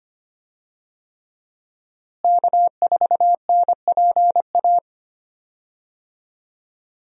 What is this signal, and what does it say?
Morse code, heard as a keyed tone, about 25 wpm, 700 Hz. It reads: K4NPA
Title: USA Call Signs - 5 Characters - Repeated 3x - 25wpm